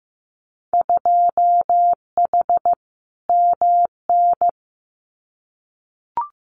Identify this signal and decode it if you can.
Morse code, heard as a keyed tone, about 15 wpm, 700 Hz. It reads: 2H MN
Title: Winter Field Day Exchanges 15wpm